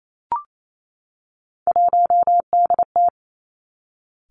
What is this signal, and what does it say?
Morse code, heard as a keyed tone, about 28 wpm, 700 Hz. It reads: 1DT
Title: Three Challenging Letter-Number Combinations 28wpm